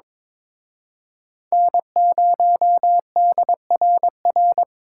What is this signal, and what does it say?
Morse code, heard as a keyed tone, about 26 wpm, 700 Hz. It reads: N0DRR
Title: Speed Racing - USA 1x3 Call Signs 22wpm